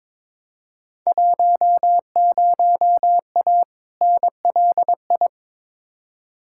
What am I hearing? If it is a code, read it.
Morse code, heard as a keyed tone, about 22 wpm, 700 Hz. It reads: 10A NLI
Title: Field Day Exchanges Encore - 22wpm